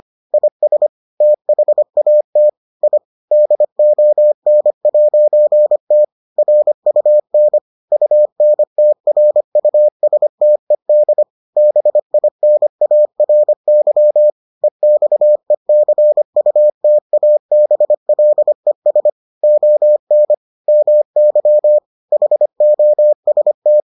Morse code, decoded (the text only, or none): IS THAT I DON'T RUN UNTRUSTED BINARY EXECUTABLES ON MY HOST